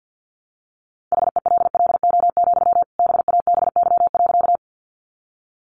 Morse code, heard as a keyed tone, about 50 wpm, 700 Hz. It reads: HELLO, BABY.
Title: Sentences from Top 1300 Words 50wpm